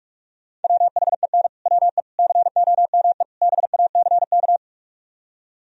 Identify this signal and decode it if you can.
Morse code, heard as a keyed tone, about 58 wpm, 700 Hz. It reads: WHEN WE COME BACK